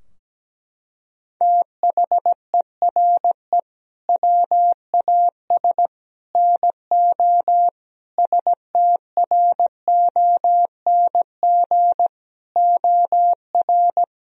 Morse code, heard as a keyed tone, about 17 wpm, 700 Hz. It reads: THERE WAS NO STRONG OR
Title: Sentences from Top 1000 Words - Review Part 4 - 17wpm